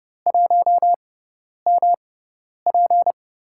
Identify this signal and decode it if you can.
Morse code, heard as a keyed tone, about 30 wpm, spaced as for 10 wpm, 700 Hz. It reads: 1MP
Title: USA Call Signs - 5 Characters - 6x Character Spacing - 30wpm